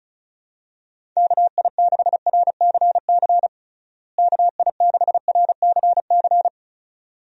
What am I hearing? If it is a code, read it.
Morse code, heard as a keyed tone, about 34 wpm, 700 Hz. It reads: KI6RCC KI6RCC
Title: USA Call Signs - 6 Characters - Repeated 2x - 35wpm